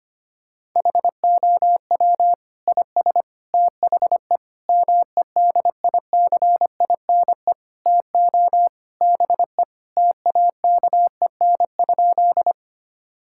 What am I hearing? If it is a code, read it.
Morse code, heard as a keyed tone, about 25 wpm, 700 Hz. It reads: HOW IS THE MEDICINE TO BE TAKEN?
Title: Sentences from Top 1000 Words - Review Part 6 - 25wpm